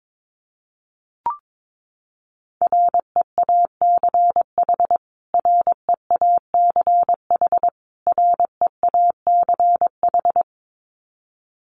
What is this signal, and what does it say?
Morse code, heard as a keyed tone, about 22 wpm, 700 Hz. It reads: REACH REACH REACH
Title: Top 500 Words Repeated 22wpm